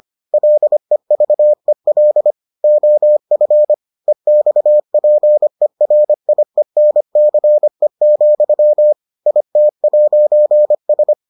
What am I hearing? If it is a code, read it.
Morse code, heard as a keyed tone, about 25 wpm, 600 Hz. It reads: LEVEL OF EXPERIENCE, IT'S